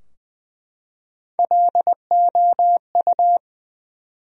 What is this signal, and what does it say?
Morse code, heard as a keyed tone, about 21 wpm, 700 Hz. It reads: LOU
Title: USA Common Names - Max 3 Letters - 20wpm